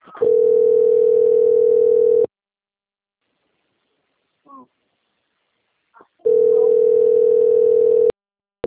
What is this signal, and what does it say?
Ringback tone (North America), 440 + 480 Hz.
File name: out- - - - - .2174